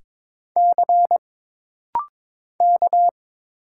Morse code, heard as a keyed tone, about 22 wpm, 700 Hz. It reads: C K
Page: This rapid-fire timing is geared to push your ICR (Instant Character Recognition) skills.